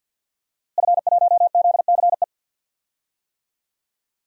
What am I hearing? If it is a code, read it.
Morse code, heard as a keyed tone, about 53 wpm, 700 Hz. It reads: F1ZCE
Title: DXCC-100 Call Signs - 5 Characters - Repeated 2x - 50wpm